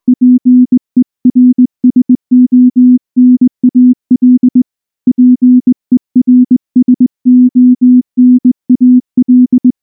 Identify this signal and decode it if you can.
Morse code, heard as a keyed tone, about 20 wpm, 250 Hz. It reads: PERSONAL PERSONAL
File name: morse / morse-training-20WPM-2